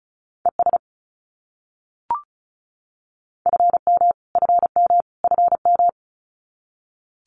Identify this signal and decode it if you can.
Morse code, heard as a keyed tone, about 34 wpm, 700 Hz. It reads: ES FM FM FM
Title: Abbreviations Repeated 3x - 35wpm